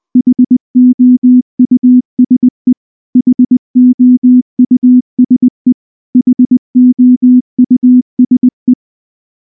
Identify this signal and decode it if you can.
Morse code, heard as a keyed tone, about 20 wpm, 250 Hz. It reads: HOUSE HOUSE HOUSE
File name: morse / morse-training-20WPM-8